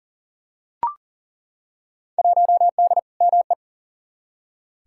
Morse code, heard as a keyed tone, about 39 wpm, 700 Hz. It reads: 1D ME
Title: Field Day Exchanges - 40wpm